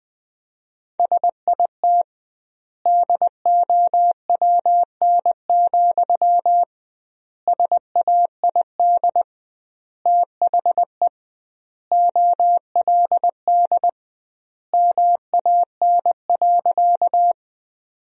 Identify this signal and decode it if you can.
Morse code, heard as a keyed tone, about 21 wpm, 700 Hz. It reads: SIT DOWN, SAID THE OLD MAN.
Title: 2x Word-Spacing - Sentences from Top 500 Words - 20wpm